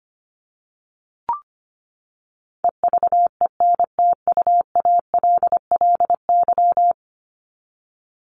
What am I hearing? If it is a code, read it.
Morse code, heard as a keyed tone, about 25 wpm, 700 Hz. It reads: EVENTUALLY
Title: LY Suffix - Top 100 Words with Base Words 25wpm